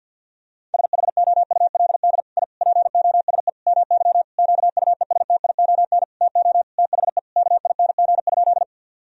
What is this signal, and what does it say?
Morse code, heard as a keyed tone, about 50 wpm, 700 Hz. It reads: SHOULD I POSE MY QUESTION TO THE KING?
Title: Sentences from Top 1000 Words - Review Part 2 - 50wpm